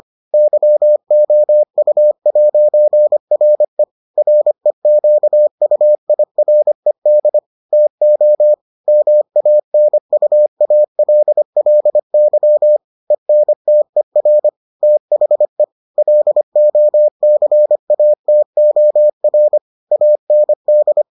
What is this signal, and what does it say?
Morse code, heard as a keyed tone, about 25 wpm, 600 Hz. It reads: YOU'RE REQUIRED TO MANUALLY ENTER THE LOCATOR AND